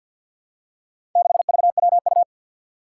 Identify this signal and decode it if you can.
Morse code, heard as a keyed tone, about 52 wpm, 700 Hz. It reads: 6VWU
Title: Four Challenging Letter-Number Combinations 50wpm